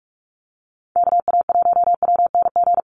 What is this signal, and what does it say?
Morse code, heard as a keyed tone, about 44 wpm, 700 Hz. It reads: KA1WNG
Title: USA Call Signs 45wpm